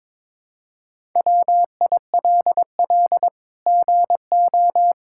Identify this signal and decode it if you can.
Morse code, heard as a keyed tone, about 22 wpm, 700 Hz. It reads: WILL GO
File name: practice / 22wpm / Sets of 7 Words - 22wpm